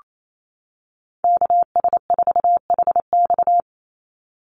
Morse code, heard as a keyed tone, about 28 wpm, 700 Hz. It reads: KS4HX
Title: USA Call Signs 28wpm